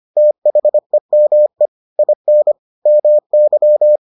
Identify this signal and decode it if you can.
Morse code, heard as a keyed tone, about 25 wpm, 600 Hz. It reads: THEME IN MY